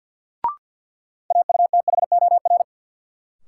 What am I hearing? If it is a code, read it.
Morse code, heard as a keyed tone, about 53 wpm, 700 Hz. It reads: AUTHOR
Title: Sentences from Top 1200 Words - New Word Drill Alt - 50wpm